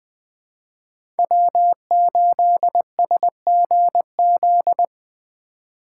Morse code, heard as a keyed tone, about 21 wpm, 700 Hz. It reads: W8SGZ
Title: USA Call Signs - 5 Characters - 2x Character Spacing - 20wpm